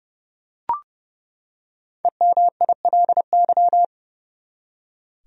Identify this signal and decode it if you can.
Morse code, heard as a keyed tone, about 32 wpm, 700 Hz. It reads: EMILY
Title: USA Common Names - 30wpm